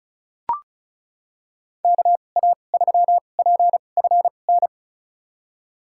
Morse code, heard as a keyed tone, about 34 wpm, spaced as for 25 wpm, 700 Hz. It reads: KA3PFN
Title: USA Call Signs - 2x Character Spacing - 35wpm